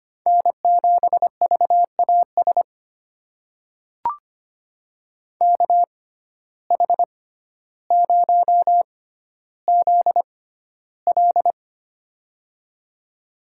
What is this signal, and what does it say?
Morse code, heard as a keyed tone, about 25 wpm, spaced as for 13 wpm, 700 Hz. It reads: N7VAS KH0ZL
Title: USA Call Signs - 5 Characters - 6x Character Spacing - 25wpm